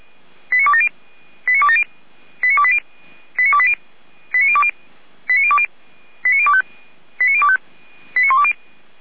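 Five-tone selective call (ZVEI), heard as quick runs of five tones, tones of about 70 ms. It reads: 89270 89270 89280 89280 89020 89020 89025 89025 89120
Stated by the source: ZVEI five-tone call received by SDR.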